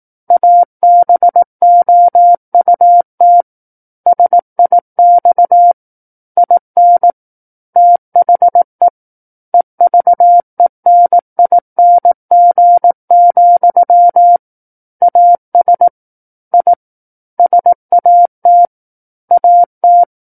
Morse code, heard as a keyed tone, about 18 wpm, 700 Hz. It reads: ABOUT SIX IN THE EVENING, AS I SAT AT
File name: 18 WPM